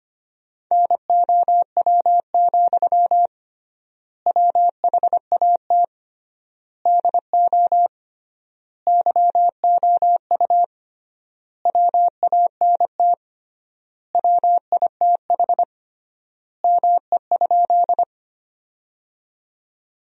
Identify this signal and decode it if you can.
Morse code, heard as a keyed tone, about 25 wpm, 700 Hz. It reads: NOW, WHAT DO YOU WANT WITH ME?
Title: 3x Word-Spacing - Sentences from Top 200 Words - 25wpm